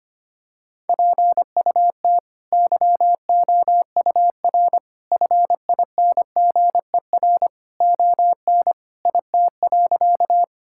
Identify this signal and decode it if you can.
Morse code, heard as a keyed tone, about 25 wpm, 700 Hz. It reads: PUT YOUR FINGER ON IT.
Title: Sentences from Top 1600 Words 25wpm